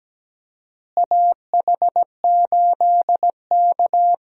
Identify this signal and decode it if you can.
Morse code, heard as a keyed tone, about 17 wpm, 700 Hz. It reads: AH8K
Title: USA Call Signs - 4 Characters - Repeated 3x - 17wpm